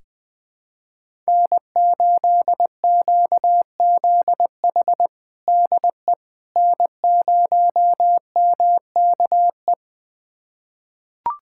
Morse code, heard as a keyed tone, about 19 wpm, 700 Hz. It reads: N8QZH DE N0MKE
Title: USA Callsign DE Callsign - 5 Characters - 20wpm